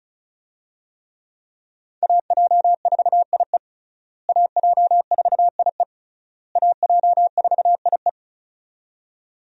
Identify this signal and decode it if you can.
Morse code, heard as a keyed tone, about 36 wpm, 700 Hz. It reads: AJ4IE AJ4IE AJ4IE
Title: USA Call Signs - 5 Characters - Repeated 3x - 35wpm